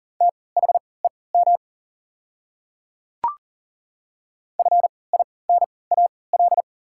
Morse code, heard as a keyed tone, about 40 wpm, spaced as for 23 wpm, 690 Hz. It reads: THEM FINAL
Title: 3x Character Spacing - Top 500 Words - 40wpm